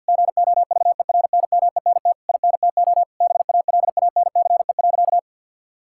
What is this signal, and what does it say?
Morse code, heard as a keyed tone, about 48 wpm, 700 Hz. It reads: GOVERNMENT INTO BALANCE.
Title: Sentences from Top 1000 Words - Review Part 6 - 50wpm